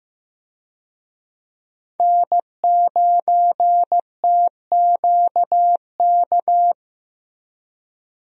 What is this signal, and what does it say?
Morse code, heard as a keyed tone, about 15 wpm, 700 Hz. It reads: N9TQK
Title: USA Call Signs - 5 Characters - Repeated 3x - 15wpm